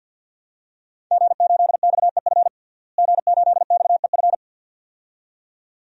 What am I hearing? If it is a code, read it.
Morse code, heard as a keyed tone, about 50 wpm, 700 Hz. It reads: G8XEF G8XEF
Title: DXCC-100 Call Signs - 5 Characters - Repeated 2x - 50wpm